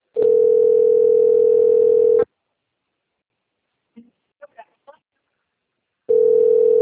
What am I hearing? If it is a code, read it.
Ringback tone (North America), 440 + 480 Hz.